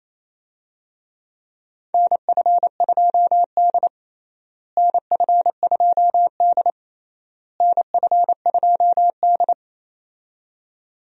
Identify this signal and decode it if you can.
Morse code, heard as a keyed tone, about 28 wpm, 700 Hz. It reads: NF2D NF2D NF2D
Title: USA Call Signs - 4 Characters - Repeated 3x - 28wpm